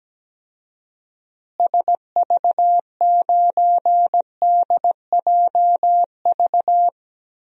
Morse code, heard as a keyed tone, about 17 wpm, 700 Hz. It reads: SV9DJV
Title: DXCC-100 Call Signs - 6 Characters - Repeated 2x - 17wpm